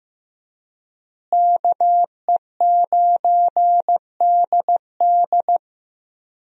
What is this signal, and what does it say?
Morse code, heard as a keyed tone, about 16 wpm, 700 Hz. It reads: KE9DD
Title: USA Call Signs - 5 Characters - Repeated 3x - 15wpm